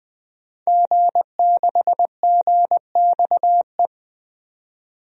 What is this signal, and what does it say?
Morse code, heard as a keyed tone, about 21 wpm, 700 Hz. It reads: G6GXE
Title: DXCC-100 Call Signs - 5 Characters - 20wpm